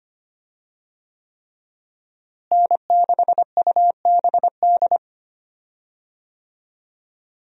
Morse code, heard as a keyed tone, about 25 wpm, 700 Hz. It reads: N6UBD
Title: USA Call Signs - 5 Characters - Repeated 3x - 25wpm